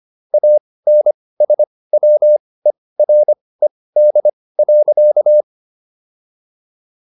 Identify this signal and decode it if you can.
Morse code, heard as a keyed tone, about 25 wpm, spaced as for 18 wpm, 600 Hz. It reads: ANSWERED.